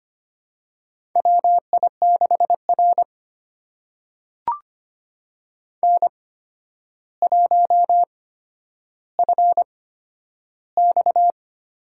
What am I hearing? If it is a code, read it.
Morse code, heard as a keyed tone, about 25 wpm, spaced as for 11 wpm, 700 Hz. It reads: WI6R N1FX
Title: USA Call Signs - 4 Characters - 8x Character Spacing - 25wpm